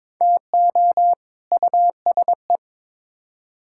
Morse code, heard as a keyed tone, about 22 wpm, 700 Hz. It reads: TO USE
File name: Build A Phrase - 2-Word Phrases - 22wpm